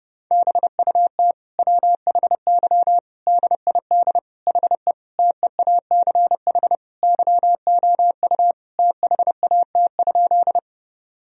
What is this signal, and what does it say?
Morse code, heard as a keyed tone, about 32 wpm, 700 Hz. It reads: BUT WHY DID HE TEACH YOU THAT?
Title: Sentences from Top 1000 Words - Review Part 2 - 30wpm